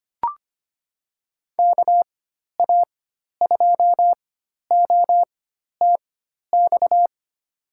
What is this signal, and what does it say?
Morse code, heard as a keyed tone, about 25 wpm, spaced as for 12 wpm, 700 Hz. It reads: KA2OTX
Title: USA Call Signs - 4x Character Spacing - 25wpm